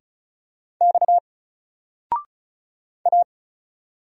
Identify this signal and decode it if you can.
Morse code, heard as a keyed tone, about 35 wpm, 700 Hz. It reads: X A
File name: Single-Letter-Number-35wpm